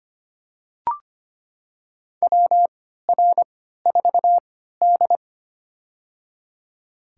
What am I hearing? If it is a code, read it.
Morse code, heard as a keyed tone, about 25 wpm, spaced as for 14 wpm, 700 Hz. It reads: WR4D
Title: USA Call Signs - 4 Characters - 3x Character Spacing - 25wpm